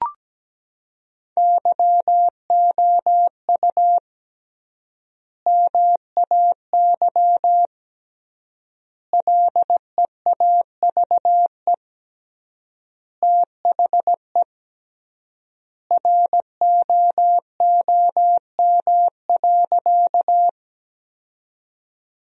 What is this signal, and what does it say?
Morse code, heard as a keyed tone, about 17 wpm, 700 Hz. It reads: YOU MAY LEAVE THE ROOM.
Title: 3x Word-Spacing - Sentences from Top 400 Words - 17wpm